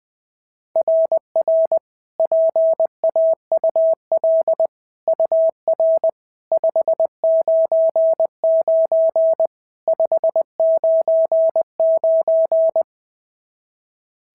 Morse code, heard as a keyed tone, about 18 wpm, 650 Hz. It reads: RR PAUL UR 599 599
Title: SOTA QSO Practice 20wpm